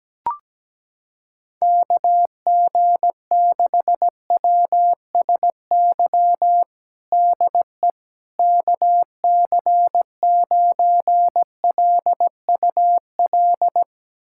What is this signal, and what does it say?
Morse code, heard as a keyed tone, about 17 wpm, 700 Hz. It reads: KG6WSY DE KC9LUL
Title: USA Callsign DE Callsign - 6 Characters - 17wpm